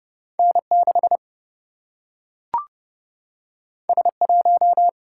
Morse code, heard as a keyed tone, about 30 wpm, 700 Hz. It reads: N 6 S 1
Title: Single-Letter Single-Number 30wpm